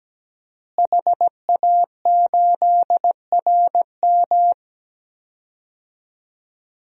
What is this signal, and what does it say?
Morse code, heard as a keyed tone, about 17 wpm, 700 Hz. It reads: HA8RM
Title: DXCC-100 Call Signs - 5 Characters - Repeated 2x - 17wpm